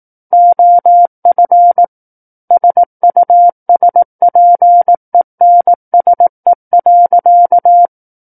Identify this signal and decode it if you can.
Morse code, heard as a keyed tone, about 18 wpm, 700 Hz. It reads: OF SUSPENSE.
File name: 18 WPM